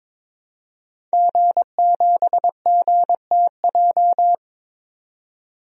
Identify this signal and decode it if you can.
Morse code, heard as a keyed tone, about 22 wpm, 700 Hz. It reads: G7GTJ
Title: DXCC-100 Call Signs - 5 Characters - 22wpm